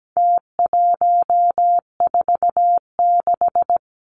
Morse code, heard as a keyed tone, about 17 wpm, 700 Hz. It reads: T146
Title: Numbers - Contest Style - 4-Digit 17wpm